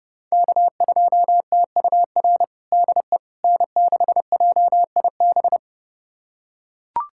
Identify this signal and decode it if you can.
Morse code, heard as a keyed tone, about 30 wpm, 700 Hz. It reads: K2TUR DE N6JIB
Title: USA Callsign DE Callsign - 5 Characters - 30wpm